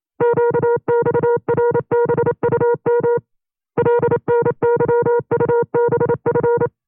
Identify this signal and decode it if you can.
Morse code, heard as a keyed tone, about 28 wpm, 480 Hz. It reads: QXRBUM LNYUBF
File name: FreeFiles / CW-practice / 25WPM-side1